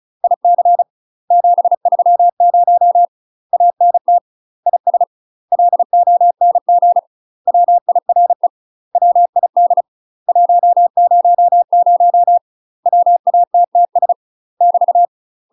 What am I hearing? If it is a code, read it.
Morse code, heard as a keyed tone, about 34 wpm, 700 Hz. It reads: IC 730 ANT IS LONG WIRE WID 100 WATTS <BT>
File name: UK MASTER QSOs TRAINING/35WPM